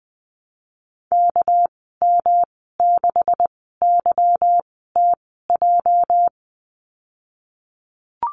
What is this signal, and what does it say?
Morse code, heard as a keyed tone, about 19 wpm, spaced as for 14 wpm, 700 Hz. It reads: KM6YTJ
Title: 213 Lesson 34g - Number 8 - Call Signs 20wpm